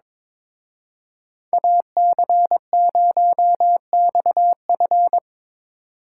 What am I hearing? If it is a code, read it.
Morse code, heard as a keyed tone, about 24 wpm, 700 Hz. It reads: AC0XF
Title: Speed Racing - USA 2x2 Call Signs 22wpm